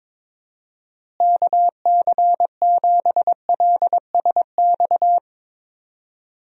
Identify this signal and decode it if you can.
Morse code, heard as a keyed tone, about 22 wpm, 700 Hz. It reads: KC7LSX
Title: USA Call Signs - 6 Characters - Repeated 3x - 22wpm